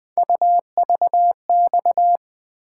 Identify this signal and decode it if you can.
Morse code, heard as a keyed tone, about 20 wpm, 700 Hz. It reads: UVX
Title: Three Letter-Number Combinations - 20wpm